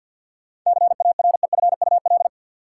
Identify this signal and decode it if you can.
Morse code, heard as a keyed tone, about 53 wpm, 700 Hz. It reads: CAREFUL
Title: Sentences from Top 1600 Words - New Word Drill Alt - 50wpm